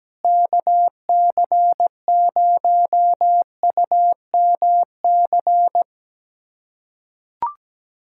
Morse code, heard as a keyed tone, about 17 wpm, 700 Hz. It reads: KC0UMC
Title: USA Callsign DE Callsign - 6 Characters - 17wpm